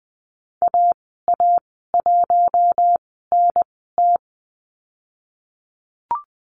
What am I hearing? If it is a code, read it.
Morse code, heard as a keyed tone, about 20 wpm, spaced as for 15 wpm, 700 Hz. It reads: AA1NT
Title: 048 Lesson 9g - Number 4 - Call Signs 20wpm